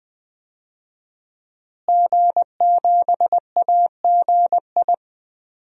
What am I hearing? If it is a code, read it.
Morse code, heard as a keyed tone, about 20 wpm, 700 Hz. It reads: G7AGI
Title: DXCC-100 Call Signs - 5 Characters - Repeated 3x - 20wpm